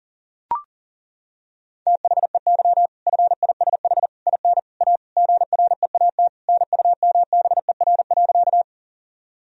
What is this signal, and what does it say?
Morse code, heard as a keyed tone, about 40 wpm, 700 Hz. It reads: THEY FISH IN A GREAT NUMBER.
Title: Sentences from Top 1000 Words - Review Part 1 - 40wpm